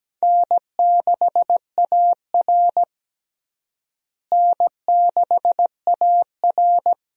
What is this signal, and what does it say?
Morse code, heard as a keyed tone, about 17 wpm, 700 Hz. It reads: N6AR N6AR
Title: USA Call Signs - 4 Characters - Repeated 2x - 17wpm